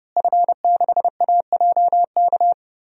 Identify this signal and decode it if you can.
Morse code, heard as a keyed tone, about 30 wpm, 700 Hz. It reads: F6AJK
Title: DXCC-100 Call Signs - 5 Characters - Repeated 3x - 30wpm